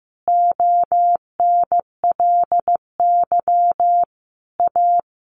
Morse code, heard as a keyed tone, about 15 wpm, 700 Hz. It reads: ONLY A
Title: Sentences from Top 1500 Words 15wpm